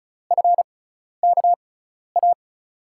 Morse code, heard as a keyed tone, about 36 wpm, spaced as for 12 wpm, 700 Hz. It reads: FKA
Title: USA Call Signs - 6 Characters - 6x Character Spacing - 35wpm